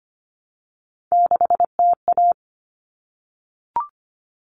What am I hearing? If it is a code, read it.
Morse code, heard as a keyed tone, about 25 wpm, 700 Hz. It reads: 6TA
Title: All Cut Numbers - 3-Digit 25wpm